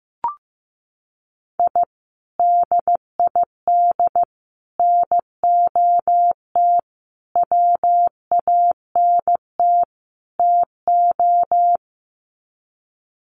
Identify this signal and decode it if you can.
Morse code, heard as a keyed tone, about 15 wpm, 700 Hz. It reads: I DID NOT WANT TO
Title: Sets of 5 Words Encore 15wpm